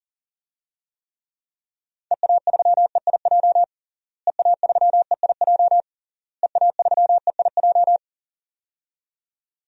Morse code, heard as a keyed tone, about 40 wpm, 700 Hz. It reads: EA3EIJ EA3EIJ EA3EIJ
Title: DXCC-100 Call Signs - 6 Characters - Repeated 3x - 40wpm